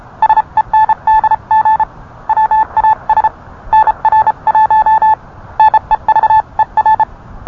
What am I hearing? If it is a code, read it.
Morse code, heard as a keyed tone, about 31 wpm, 870 Hz. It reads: SENDG WAS NR1 NEVER